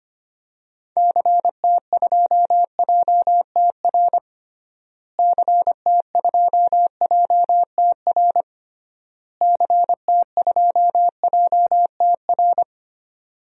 Morse code, heard as a keyed tone, about 25 wpm, 700 Hz. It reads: CT2JTR CT2JTR CT2JTR
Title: DXCC-100 Call Signs - 6 Characters - Repeated 3x - 25wpm